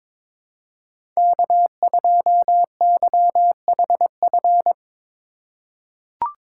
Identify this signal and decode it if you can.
Morse code, heard as a keyed tone, about 22 wpm, 700 Hz. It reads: K2YHF
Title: USA Call Signs - 5 Characters - 3x Character Spacing - 22wpm